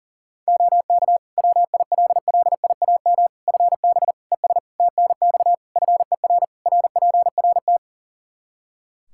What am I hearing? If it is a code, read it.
Morse code, heard as a keyed tone, about 40 wpm, 700 Hz. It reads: OK WILLIAM FB ES TNX FER RPRT
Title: Practice QSO - Exchange 2 - Info - 40wpm